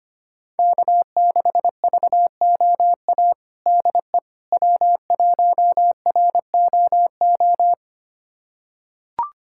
Morse code, heard as a keyed tone, about 25 wpm, 700 Hz. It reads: K6VOA DE W1ROO
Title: USA Callsign DE Callsign - 5 Characters - 25wpm